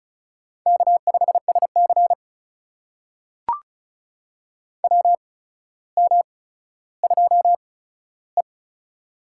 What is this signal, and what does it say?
Morse code, heard as a keyed tone, about 36 wpm, spaced as for 15 wpm, 700 Hz. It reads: K5SC WM2E
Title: USA Call Signs - 4 Characters - 8x Character Spacing - 35wpm